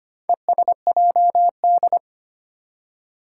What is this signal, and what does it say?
Morse code, heard as a keyed tone, about 25 wpm, 700 Hz. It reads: ESJD
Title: Four Challenging Letter-Number Combinations 25wpm